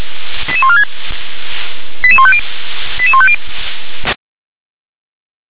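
Five-tone selective call (ZVEI), heard as quick runs of five tones, tones of about 70 ms. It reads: 88146 88150 88150
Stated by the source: ZVEI five-tone selective call received by SDR.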